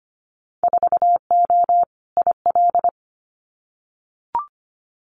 Morse code, heard as a keyed tone, about 25 wpm, 700 Hz. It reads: 4O IL
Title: Winter Field Day Exchanges Encore 25wpm